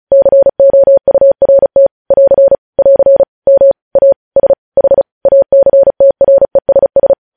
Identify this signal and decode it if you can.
Morse code, heard as a keyed tone, about 36 wpm, 550 Hz. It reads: COURT <AR> <AR> MASH ACTRESS
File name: podcast / News headlines-1x-35x35